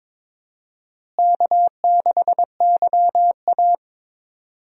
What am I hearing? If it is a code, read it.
Morse code, heard as a keyed tone, about 22 wpm, 700 Hz. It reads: K6YA
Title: USA Call Signs - 4 Characters - Repeated 2x - 22wpm